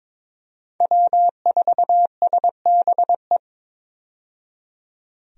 Morse code, heard as a keyed tone, about 23 wpm, 700 Hz. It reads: W4SBE
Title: USA Call Signs - 5 Characters - 22wpm